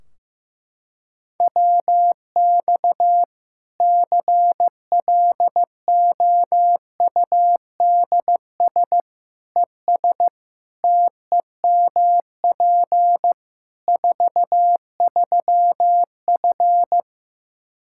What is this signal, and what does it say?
Morse code, heard as a keyed tone, about 15 wpm, 700 Hz. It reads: WX CLOUDS ES TEMP 43F
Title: K1USN SST QSO Practice 15wpm